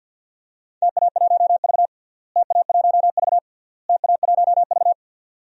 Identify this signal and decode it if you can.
Morse code, heard as a keyed tone, about 50 wpm, 700 Hz. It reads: TA1V TA1V TA1V
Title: DXCC-100 Call Signs - 4 Characters - Repeated 3x - 50wpm